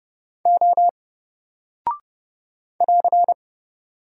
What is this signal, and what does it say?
Morse code, heard as a keyed tone, about 30 wpm, 700 Hz. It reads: O <AR>
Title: Single Character (Letter:Number:Punctuation:Prosign) 30wpm